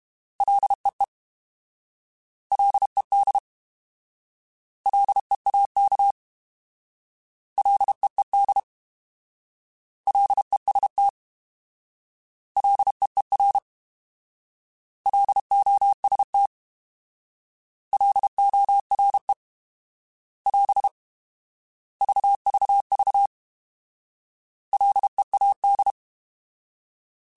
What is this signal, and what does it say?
Morse code, heard as a keyed tone, about 32 wpm, 800 Hz. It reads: LEE LED LEAK LEED LEST LEER LOST LORE <AS> VVV LEAD